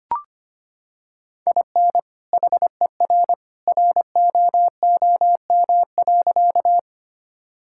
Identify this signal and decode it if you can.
Morse code, heard as a keyed tone, about 25 wpm, 700 Hz. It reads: IN HER ROOM.
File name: practice / 25wpm / Sentences with 2-Word Phrases - 25wpm